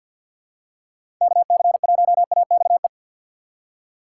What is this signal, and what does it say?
Morse code, heard as a keyed tone, about 52 wpm, 690 Hz. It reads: KC1AXE
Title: USA Call Signs - 6 Characters - 3x Character Spacing - 50wpm